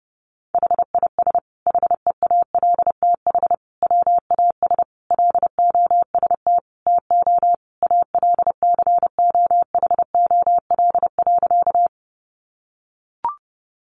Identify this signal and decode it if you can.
Morse code, heard as a keyed tone, about 30 wpm, 700 Hz. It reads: HIS HEALTH WAS LOST TO ALCOHOL.
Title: Sentences from Top 1400 Words 30wpm